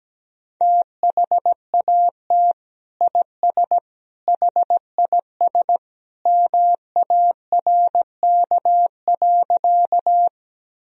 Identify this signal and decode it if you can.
Morse code, heard as a keyed tone, about 17 wpm, 700 Hz. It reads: THAT IS HIS MARK.
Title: Sentences from Top 1000 Words - Review Part 1 - 17wpm